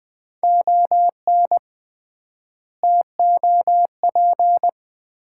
Morse code, heard as a keyed tone, about 20 wpm, 700 Hz. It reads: ON TOP
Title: 3x Word-Spacing - Sets of 4 Words - 20wpm